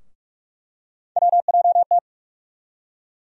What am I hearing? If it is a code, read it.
Morse code, heard as a keyed tone, about 41 wpm, 700 Hz. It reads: WJT
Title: USA Call Sign Postfixes - 3 Letters 45wpm